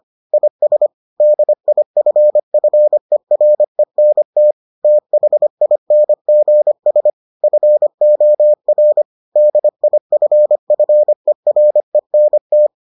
Morse code, heard as a keyed tone, about 25 wpm, 600 Hz. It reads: IS DIFFERENT THINGS FOR DIFFERENT